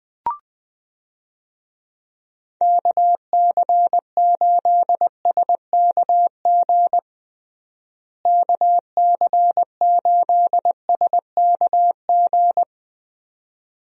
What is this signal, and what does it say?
Morse code, heard as a keyed tone, about 20 wpm, 700 Hz. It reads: KC8SKG KC8SKG
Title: USA Call Signs - 6 Characters - Repeated 2x - 20wpm